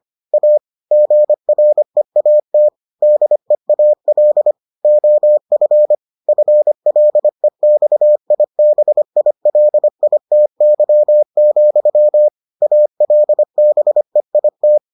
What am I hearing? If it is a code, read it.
Morse code, heard as a keyed tone, about 25 wpm, 600 Hz. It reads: A GREAT DEAL OF FLEXIBILITY, ALBEIT